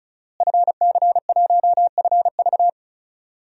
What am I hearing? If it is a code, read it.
Morse code, heard as a keyed tone, about 34 wpm, 700 Hz. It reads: FC1FV
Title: DXCC-100 Call Signs - 5 Characters - Repeated 3x - 35wpm